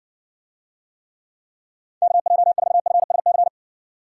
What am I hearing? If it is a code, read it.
Morse code, heard as a keyed tone, about 60 wpm, 700 Hz. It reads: KJ4FSC